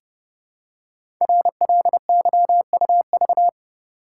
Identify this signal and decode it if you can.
Morse code, heard as a keyed tone, about 28 wpm, 700 Hz. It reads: RLYUV
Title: Code Groups - 5 Letters - 30wpm